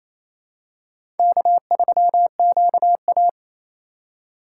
Morse code, heard as a keyed tone, about 27 wpm, 700 Hz. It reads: K3QA
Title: USA Call Signs - 3x Character Spacing - 28wpm